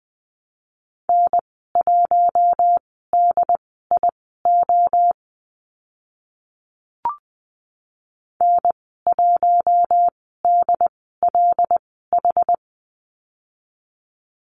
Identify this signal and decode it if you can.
Morse code, heard as a keyed tone, about 20 wpm, spaced as for 15 wpm, 700 Hz. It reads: N1DIO N1DLH
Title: 082 Lesson 14g - Number 2 - Call Signs 20wpm